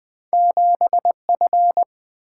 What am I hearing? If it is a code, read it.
Morse code, heard as a keyed tone, about 20 wpm, 700 Hz. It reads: 7 F
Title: Single-Letter Single-Number 20wpm